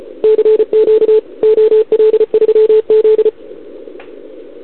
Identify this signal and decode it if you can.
Morse code, heard as a keyed tone, about 34 wpm, 440 Hz. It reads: CQ OL3Z
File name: CQWWCW2021 / OL3Z